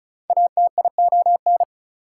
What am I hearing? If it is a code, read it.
Morse code, heard as a keyed tone, about 36 wpm, 700 Hz. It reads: ATION
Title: Prefixes and Suffixes 35wpm